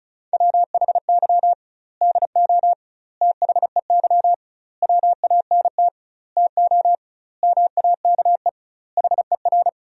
Morse code, heard as a keyed tone, about 34 wpm, 700 Hz. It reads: WHY DO THEY WANT TO MAKE HER
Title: 2x Word-Spacing - Sentences from Top 400 Words - 35wpm